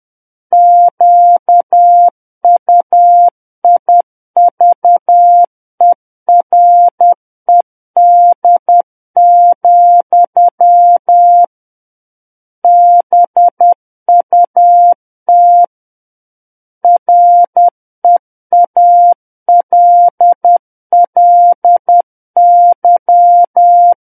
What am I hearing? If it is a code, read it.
Morse code, heard as a keyed tone, about 10 wpm, 700 Hz. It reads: QUIVERED, BUT REALLY
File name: Book One / 01-Chapter 1 - 10 WPM